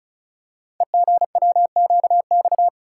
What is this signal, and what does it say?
Morse code, heard as a keyed tone, about 34 wpm, 700 Hz. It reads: EGWQX
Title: Code Groups - 5 Letters - 35wpm